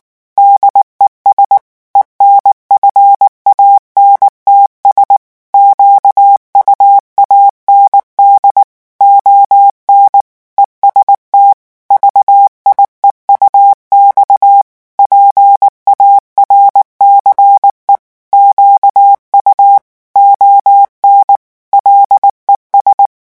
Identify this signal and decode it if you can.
Morse code, heard as a keyed tone, about 19 wpm, 800 Hz. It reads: DES ENFANTS QUAND ON EST VIEUX PARCE QU ON LES
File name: morse mp3 / revision9 citations humoristiques vitesse 18 mots minute